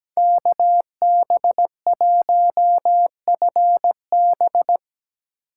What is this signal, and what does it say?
Morse code, heard as a keyed tone, about 17 wpm, 700 Hz. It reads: KB1FB
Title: USA Call Signs - 5 Characters - 2x Character Spacing - 17wpm